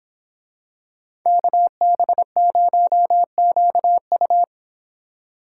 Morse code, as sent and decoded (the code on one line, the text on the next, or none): -.- -... ----- --.- ..-
KB0QU